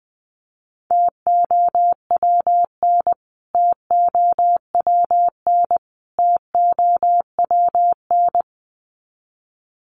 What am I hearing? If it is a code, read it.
Morse code, heard as a keyed tone, about 20 wpm, 700 Hz. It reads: TOWN TOWN TOWN
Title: Top 500 Words Repeated 20wpm